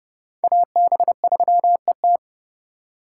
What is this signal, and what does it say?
Morse code, heard as a keyed tone, about 30 wpm, 700 Hz. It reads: AB3ET
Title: USA Call Signs - 5 Characters - 3x Character Spacing - 30wpm